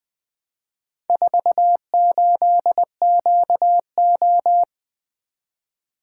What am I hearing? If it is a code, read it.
Morse code, heard as a keyed tone, about 20 wpm, 700 Hz. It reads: 48QO
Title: Four Letter-Number Combinations - 20wpm